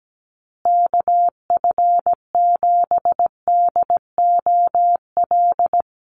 Morse code, heard as a keyed tone, about 17 wpm, 700 Hz. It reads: KF7DOL
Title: USA Call Signs Repeated 2x 17wpm